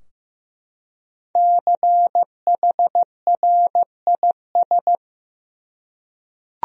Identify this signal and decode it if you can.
Morse code, heard as a keyed tone, about 15 wpm, 700 Hz. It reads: CHRIS
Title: USA Common Names - 15wpm